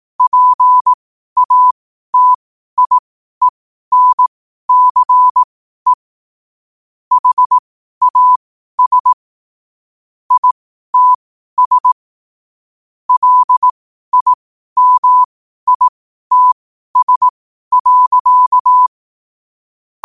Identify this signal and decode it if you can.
Morse code, heard as a keyed tone, about 18 wpm, spaced as for 12 wpm, 1000 Hz. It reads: PATIENCE HAS ITS LIMITS.
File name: qotdfiles / 12wpm